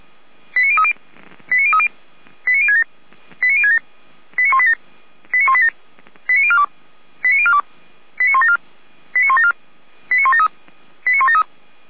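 Five-tone selective call (ZVEI), heard as quick runs of five tones, tones of about 70 ms. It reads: 89030 89030 89067 89067 89187 89187 89042 89042 89174 89174 89173 89173
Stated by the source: SDR-received ZVEI five-tone selective call.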